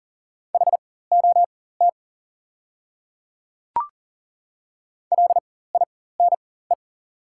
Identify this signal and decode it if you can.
Morse code, heard as a keyed tone, about 40 wpm, spaced as for 19 wpm, 700 Hz. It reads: HOT LINE
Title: 4x Character Spacing - Top 200 Words - 40wpm